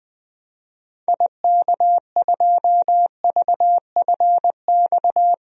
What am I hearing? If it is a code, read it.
Morse code, heard as a keyed tone, about 21 wpm, 700 Hz. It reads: IK2VFX
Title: DXCC-100 Call Signs - 6 Characters - Repeated 2x - 20wpm